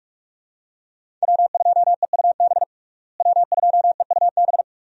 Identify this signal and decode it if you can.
Morse code, heard as a keyed tone, about 42 wpm, 700 Hz. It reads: W2EUB W2EUB
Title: USA Call Signs - 5 Characters - Repeated 2x - 45wpm